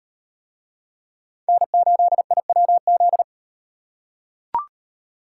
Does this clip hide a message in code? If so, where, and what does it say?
Yes: Morse code from 1.5 to 3.2 s, heard as a keyed tone, about 35 wpm, 700 Hz. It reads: N8IWZ